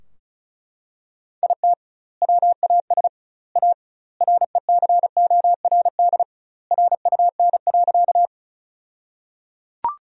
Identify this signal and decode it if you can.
Morse code, heard as a keyed tone, about 34 wpm, 700 Hz. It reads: IT WAS A RECORD RUN.
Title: 2x Word-Spacing - Sentences from Top 500 Words - 35wpm